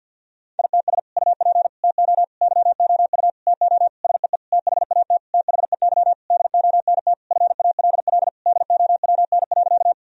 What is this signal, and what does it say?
Morse code, heard as a keyed tone, about 48 wpm, 700 Hz. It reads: ITS UP TO YOU TO SEE THAT THEY DONT FALL DOWN.
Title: Sentences from Top 1000 Words - Review Part 2 - 50wpm